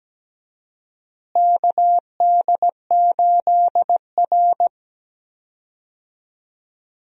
Morse code, heard as a keyed tone, about 17 wpm, 700 Hz. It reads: KD8R
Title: USA Call Signs - 4 Characters - Repeated 3x - 17wpm